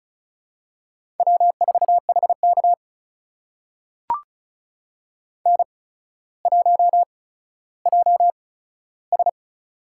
Morse code, heard as a keyed tone, about 36 wpm, spaced as for 15 wpm, 700 Hz. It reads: W4HK N1JS
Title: USA Call Signs - 8x Character Spacing - 35wpm